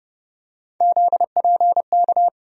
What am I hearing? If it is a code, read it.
Morse code, heard as a keyed tone, about 30 wpm, 700 Hz. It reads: ZPK
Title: USA Call Sign Postfixes - 3 Letters 30wpm